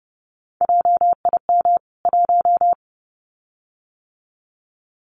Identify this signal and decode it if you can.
Morse code, heard as a keyed tone, about 30 wpm, 700 Hz. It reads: JIM 1
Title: CWT Exchange 30wpm